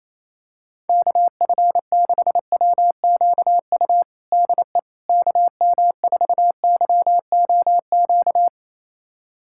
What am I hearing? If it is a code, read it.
Morse code, heard as a keyed tone, about 28 wpm, 700 Hz. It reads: KF6WQU DE KM4YOQ
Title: USA Callsign DE Callsign - 6 Characters - 28wpm